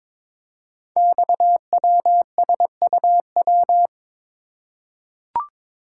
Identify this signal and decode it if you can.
Morse code, heard as a keyed tone, about 22 wpm, 700 Hz. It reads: XWSUW
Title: Code Groups - 5 Letters - 22wpm